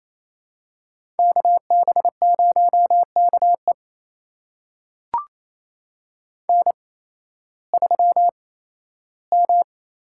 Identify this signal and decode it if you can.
Morse code, heard as a keyed tone, about 28 wpm, spaced as for 15 wpm, 700 Hz. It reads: KB0KE N3M
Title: USA Call Signs - 5 Characters - 8x Character Spacing - 28wpm